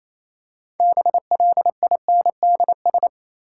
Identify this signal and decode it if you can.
Morse code, heard as a keyed tone, about 27 wpm, 700 Hz. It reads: BLINDS
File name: Easy Words - Speed-Racing with Hints - 22wpm